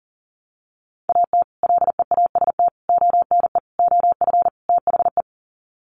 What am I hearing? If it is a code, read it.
Morse code, heard as a keyed tone, about 40 wpm, 700 Hz. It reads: AT LEAST ONE OF THE
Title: Sets of 5 Words Encore 40wpm